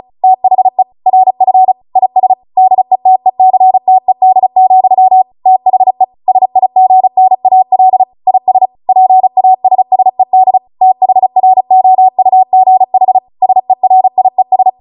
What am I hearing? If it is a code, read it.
Morse code, heard as a keyed tone, about 36 wpm, 750 Hz. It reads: THE RF IS DETECTED, THE SIGNAL IS PASSED THROUGH SERIES